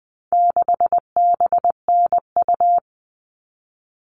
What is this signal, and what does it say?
Morse code, heard as a keyed tone, about 20 wpm, 700 Hz. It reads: 6BNU
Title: All Cut Numbers - 4-Digit 20wpm